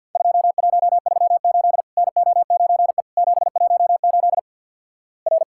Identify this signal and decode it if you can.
Morse code, heard as a keyed tone, about 48 wpm, 700 Hz. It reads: 2138 NO9E 718 R
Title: Worked All Europe DX Contest - QTC Exchange - 50wpm